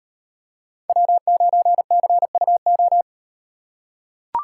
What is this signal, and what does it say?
Morse code, heard as a keyed tone, about 37 wpm, 700 Hz. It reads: W9CUO